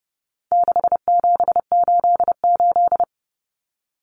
Morse code, heard as a keyed tone, about 29 wpm, 700 Hz. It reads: 6788
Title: Numbers - 4-Digit 30wpm